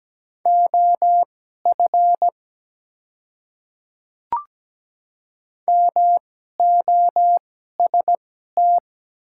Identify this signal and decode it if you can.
Morse code, heard as a keyed tone, about 17 wpm, spaced as for 12 wpm, 700 Hz. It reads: OF MOST
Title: 2x Character Spacing - Top 100 Words - 17wpm